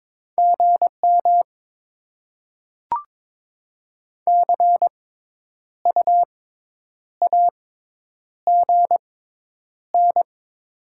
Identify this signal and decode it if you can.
Morse code, heard as a keyed tone, about 22 wpm, spaced as for 9 wpm, 700 Hz. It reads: GM CUAGN
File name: Top 20 QSO Elements - 6x Character Spacing - 22wpm